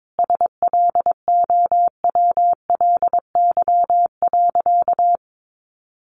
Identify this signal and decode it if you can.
Morse code, heard as a keyed tone, about 22 wpm, 700 Hz. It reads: SLOWLY.
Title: Sentences from Top 1100 Words 22wpm